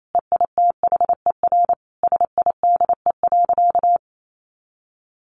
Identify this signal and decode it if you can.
Morse code, heard as a keyed tone, about 28 wpm, 700 Hz. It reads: EITHER SIDE.
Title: Sentences from Top 1200 Words 28wpm